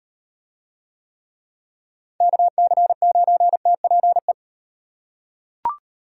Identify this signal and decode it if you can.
Morse code, heard as a keyed tone, about 35 wpm, 700 Hz. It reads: KC9TPE